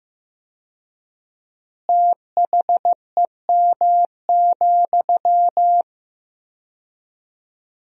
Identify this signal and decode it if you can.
Morse code, heard as a keyed tone, about 15 wpm, 700 Hz. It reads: THEM,
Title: 4x Word-Spacing - Sentences from Top 400 Words - 15wpm